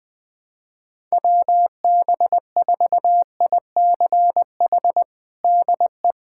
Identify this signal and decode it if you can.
Morse code, heard as a keyed tone, about 20 wpm, 700 Hz. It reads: WB4ICH DE
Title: USA Callsign DE Callsign - 6 Characters - 20wpm